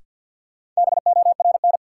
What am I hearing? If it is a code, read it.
Morse code, heard as a keyed tone, about 52 wpm, 700 Hz. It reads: BORN
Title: Sentences from Top 1000 Words - New Word Drill - 50wpm